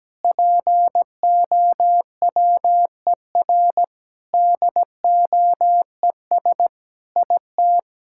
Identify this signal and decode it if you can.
Morse code, heard as a keyed tone, about 17 wpm, 700 Hz. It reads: POWER DOES IT
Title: Sentences from Top 1000 Words - Review Part 6 - 17wpm